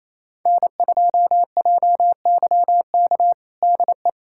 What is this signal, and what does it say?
Morse code, heard as a keyed tone, about 28 wpm, 700 Hz. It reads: N2JYK DE
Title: USA Callsign DE Callsign - 5 Characters - 28wpm